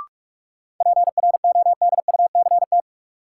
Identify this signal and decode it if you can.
Morse code, heard as a keyed tone, about 46 wpm, 700 Hz. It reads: PRODUCT
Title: Sentences from Top 1400 Words - New Word Drill Alt - 45wpm